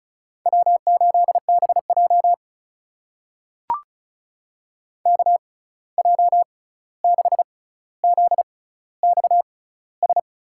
Morse code, heard as a keyed tone, about 36 wpm, spaced as for 16 wpm, 700 Hz. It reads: W8BJ KJ6ZXS
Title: USA Call Signs - 6x Character Spacing - 35wpm